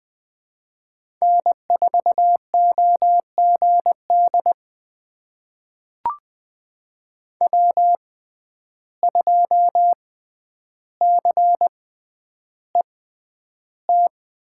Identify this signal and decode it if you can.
Morse code, heard as a keyed tone, about 20 wpm, spaced as for 10 wpm, 700 Hz. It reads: N4OGD W2CET
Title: USA Call Signs - 5 Characters - 6x Character Spacing - 20wpm